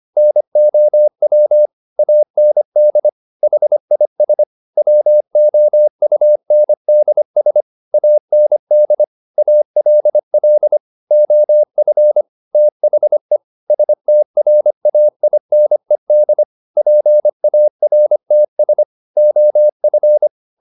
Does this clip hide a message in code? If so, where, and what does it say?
Yes: Morse code from 0.2 to 20.3 s, heard as a keyed tone, about 25 wpm, 600 Hz. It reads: NOW AND HIS WOUNDS AND ALL OF THE STRAINED PARTS OF